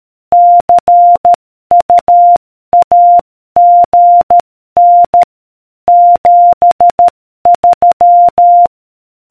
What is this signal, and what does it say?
Morse code, heard as a keyed tone, about 13 wpm, 700 Hz. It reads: CU AGN 73